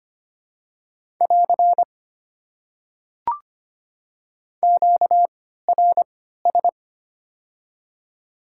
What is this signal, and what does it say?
Morse code, heard as a keyed tone, about 25 wpm, spaced as for 14 wpm, 700 Hz. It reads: <AR> QRS
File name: Top 90 QSO Elements - 3x Character Spacing - 25wpm